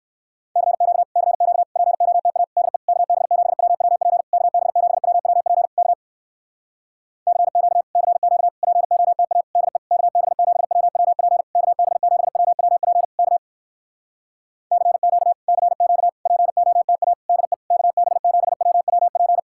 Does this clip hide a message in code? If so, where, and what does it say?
Yes: Morse code from 0.6 to 19.4 s, heard as a keyed tone, about 65 wpm, 700 Hz. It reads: CQ CQ POTA DE KB7WWP KB7WWP K CQ CQ POTA DE KB7WWP KB7WWP K CQ CQ POTA DE KB7WWP